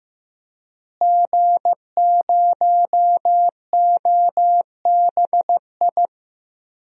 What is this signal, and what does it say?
Morse code, heard as a keyed tone, about 15 wpm, 700 Hz. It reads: G0OBI
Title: DXCC-100 Call Signs - 5 Characters - Repeated 2x - 15wpm